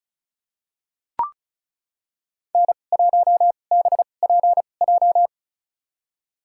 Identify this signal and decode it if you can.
Morse code, heard as a keyed tone, about 34 wpm, spaced as for 25 wpm, 700 Hz. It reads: N1BPJ
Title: USA Call Signs - 2x Character Spacing - 35wpm